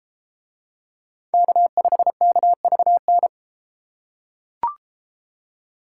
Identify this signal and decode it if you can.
Morse code, heard as a keyed tone, about 32 wpm, 700 Hz. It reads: K5KVN